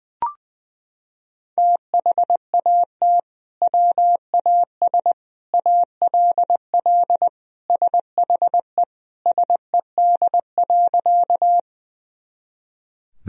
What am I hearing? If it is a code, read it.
Morse code, heard as a keyed tone, about 20 wpm, 700 Hz. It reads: THAT WAS ALL SHE SED.
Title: Sentences from Top 100 Words and Abbreviations 20wpm